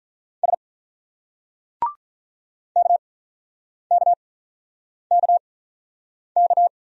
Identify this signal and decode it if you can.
Morse code, heard as a keyed tone, about 46 wpm, spaced as for 9 wpm, 700 Hz. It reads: S KKKK